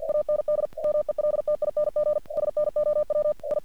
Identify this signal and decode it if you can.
Morse code, heard as a keyed tone, about 48 wpm, 600 Hz. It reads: AND MELTING SNOW I
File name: archived / CRSnet